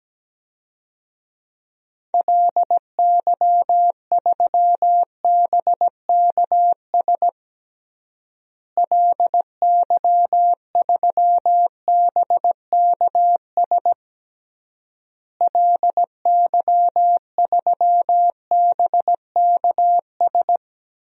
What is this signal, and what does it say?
Morse code, heard as a keyed tone, about 17 wpm, 700 Hz. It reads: LY3BKS LY3BKS LY3BKS
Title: DXCC-100 Call Signs - 6 Characters - Repeated 3x - 17wpm